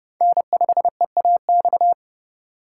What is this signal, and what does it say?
Morse code, heard as a keyed tone, about 34 wpm, 700 Hz. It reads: N5EAX